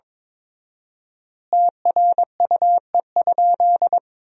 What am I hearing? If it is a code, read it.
Morse code, heard as a keyed tone, about 22 wpm, 700 Hz. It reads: TRUE?
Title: 4x Word-Spacing - Sentences from Top 400 Words - 22wpm